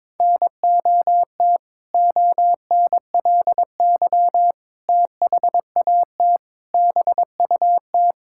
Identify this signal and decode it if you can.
Morse code, heard as a keyed tone, about 22 wpm, 700 Hz. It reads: NOT ONLY THAT BUT
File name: practice / 22wpm / Sentences with 2-Word Phrases - 22wpm